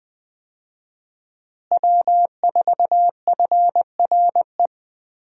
Morse code, heard as a keyed tone, about 20 wpm, 700 Hz. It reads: W4FRE
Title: USA Call Signs - 5 Characters - Repeated 3x - 20wpm